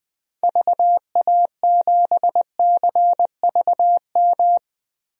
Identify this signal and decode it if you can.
Morse code, heard as a keyed tone, about 20 wpm, 700 Hz. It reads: VA7CVM
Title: DXCC-100 Call Signs - 6 Characters - Repeated 2x - 20wpm